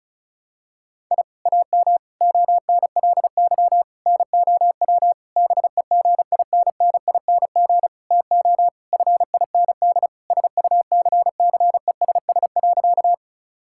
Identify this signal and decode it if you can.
Morse code, heard as a keyed tone, about 36 wpm, 700 Hz. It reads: I AM ONLY NOW BEGINNING TO FIND SUCCESS.
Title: Sentences from Top 1000 Words - Review Part 5 - 35wpm